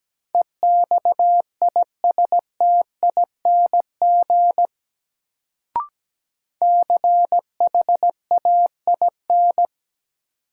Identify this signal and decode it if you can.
Morse code, heard as a keyed tone, about 17 wpm, 700 Hz. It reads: EXISTING CHAIN
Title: Sentences from Top 1500 Words - New Word Drill - 17wpm